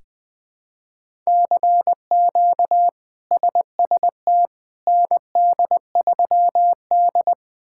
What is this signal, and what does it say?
Morse code, heard as a keyed tone, about 20 wpm, 700 Hz. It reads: CQ SST ND3D
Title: K1USN SST QSO Practice - 20wpm